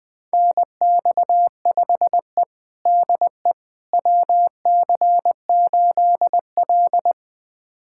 Morse code, heard as a keyed tone, about 21 wpm, 700 Hz. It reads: NX5E DE WC8L
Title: USA Callsign DE Callsign - 4 Characters - 20wpm